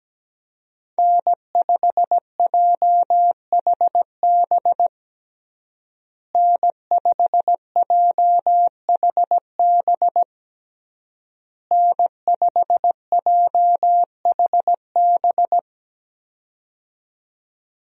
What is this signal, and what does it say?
Morse code, heard as a keyed tone, about 17 wpm, 700 Hz. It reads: N5JHB N5JHB N5JHB
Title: USA Call Signs - 5 Characters - Repeated 3x - 17wpm